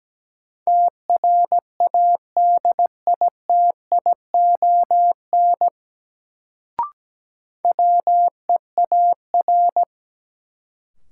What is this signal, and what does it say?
Morse code, heard as a keyed tone, about 17 wpm, 700 Hz. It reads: TRADITION WEAR
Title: Sentences from Top 1500 Words - New Word Drill Alt - 17wpm